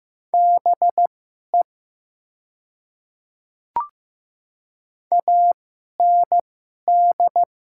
Morse code, heard as a keyed tone, about 15 wpm, spaced as for 11 wpm, 700 Hz. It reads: BE AND
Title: 2x Character Spacing - Top 100 Words - 15wpm